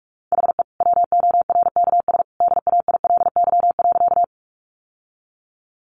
Morse code, heard as a keyed tone, about 46 wpm, 700 Hz. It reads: HE WORKS DAILY.
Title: Sentences from Top 1100 Words 45wpm